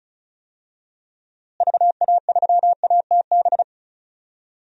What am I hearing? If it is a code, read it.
Morse code, heard as a keyed tone, about 36 wpm, 700 Hz. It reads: VA3ATB
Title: DXCC-100 Call Signs - 6 Characters - Repeated 3x - 35wpm